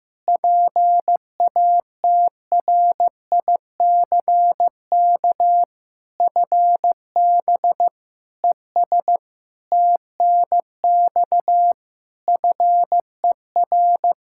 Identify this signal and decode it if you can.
Morse code, heard as a keyed tone, about 15 wpm, 700 Hz. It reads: PATRICK FB ES TNX FER
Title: Practice QSO - Exchange 2 - Info - 15wpm